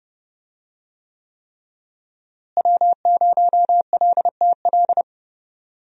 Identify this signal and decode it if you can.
Morse code, heard as a keyed tone, about 30 wpm, 700 Hz. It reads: W0LTL
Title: USA Call Signs - 5 Characters - Repeated 3x - 30wpm